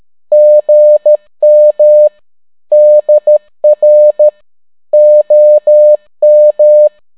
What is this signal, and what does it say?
Morse code, heard as a keyed tone, about 13 wpm, 600 Hz. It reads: GM DR OM